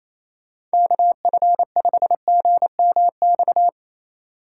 Morse code, heard as a keyed tone, about 28 wpm, 700 Hz. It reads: KF5GMX
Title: USA Call Signs - 6 Characters - 3x Character Spacing - 28wpm